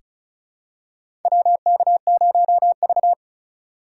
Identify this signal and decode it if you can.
Morse code, heard as a keyed tone, about 36 wpm, 690 Hz. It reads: WK0V
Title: USA Call Signs - 4 Characters - 2x Character Spacing - 35wpm